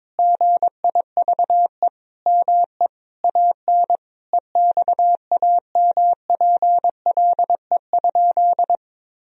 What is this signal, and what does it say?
Morse code, heard as a keyed tone, about 21 wpm, 700 Hz. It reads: GIVE ME AN EXAMPLE?
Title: Sentences from Top 1000 Words - Review Part 1 - 22wpm